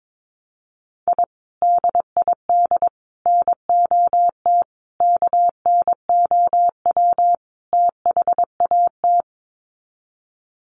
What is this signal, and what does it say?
Morse code, heard as a keyed tone, about 22 wpm, 700 Hz. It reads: I DID NOT KNOW THAT
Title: Sets of 5 Words 22wpm